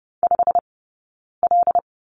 Morse code, heard as a keyed tone, about 30 wpm, spaced as for 9 wpm, 690 Hz. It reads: 5L
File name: course / 30:10wpm / 086 Lesson 15d - Number 5 - Two Characters 30wpm